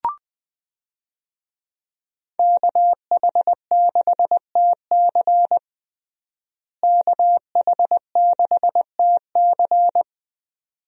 Morse code, heard as a keyed tone, about 20 wpm, 700 Hz. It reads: KH6TC KH6TC
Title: DXCC-100 Call Signs - 5 Characters - Repeated 2x - 20wpm